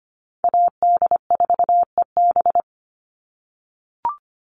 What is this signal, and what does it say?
Morse code, heard as a keyed tone, about 25 wpm, 700 Hz. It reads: AD4EB
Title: Call Signs - 25wpm